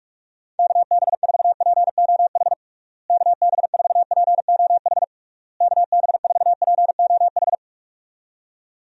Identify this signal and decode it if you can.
Morse code, heard as a keyed tone, about 41 wpm, 700 Hz. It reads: KB4POH KB4POH KB4POH
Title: USA Call Signs - 6 Characters - Repeated 3x - 45wpm